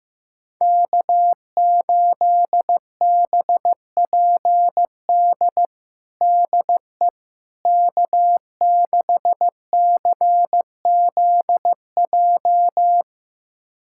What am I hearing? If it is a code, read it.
Morse code, heard as a keyed tone, about 15 wpm, 700 Hz. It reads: K8BPD DE K6CZJ
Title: USA Callsign DE Callsign - 5 Characters - 15wpm